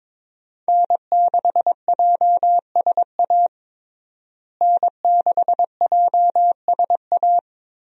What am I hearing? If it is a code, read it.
Morse code, heard as a keyed tone, about 22 wpm, 700 Hz. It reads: N6JSA N6JSA
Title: USA Call Signs - 5 Characters - Repeated 3x - 22wpm